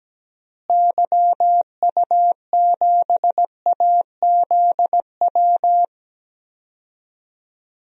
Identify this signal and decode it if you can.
Morse code, heard as a keyed tone, about 17 wpm, 700 Hz. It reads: YU7AZW
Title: DXCC-100 Call Signs - 6 Characters - Repeated 3x - 17wpm